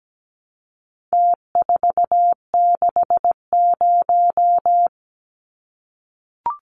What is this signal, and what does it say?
Morse code, heard as a keyed tone, about 17 wpm, 700 Hz. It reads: T460
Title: Numbers - Contest Style - 4-Digit 17wpm